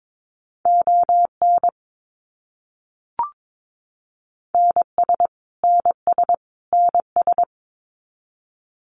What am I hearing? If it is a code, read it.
Morse code, heard as a keyed tone, about 22 wpm, 700 Hz. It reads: ON NS NS NS
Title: Canadian Provinces and Territories - Repeated 3x - 22wpm